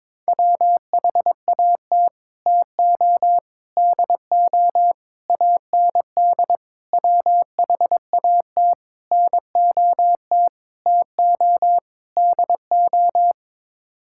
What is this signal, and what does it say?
Morse code, heard as a keyed tone, about 22 wpm, 700 Hz. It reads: WHAT TO DO AND WHAT NOT TO DO
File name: practice / 22wpm / Sets of 8 Words - 22wpm